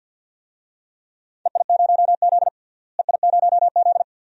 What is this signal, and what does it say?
Morse code, heard as a keyed tone, about 47 wpm, 700 Hz. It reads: EI0Z EI0Z
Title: DXCC-100 Call Signs - 4 Characters - Repeated 2x - 50wpm